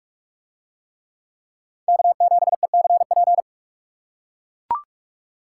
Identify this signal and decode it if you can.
Morse code, heard as a keyed tone, about 40 wpm, 700 Hz. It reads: K7ECP